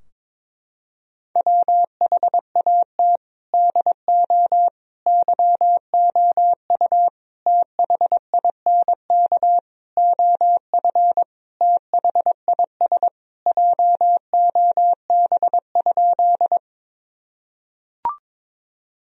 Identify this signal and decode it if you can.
Morse code, heard as a keyed tone, about 23 wpm, 700 Hz. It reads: WHAT DO YOU THINK OF THIS JOB?
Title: Sentences from Top 1000 Words - Review Part 3 - 22wpm